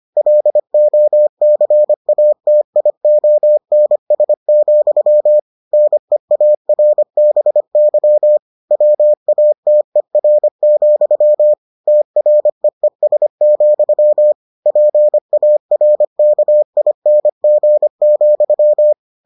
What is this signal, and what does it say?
Morse code, heard as a keyed tone, about 25 wpm, 600 Hz. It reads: LOCATIONS, NEARBY WATER, TREES, PARKING,